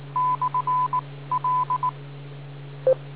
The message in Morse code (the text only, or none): / L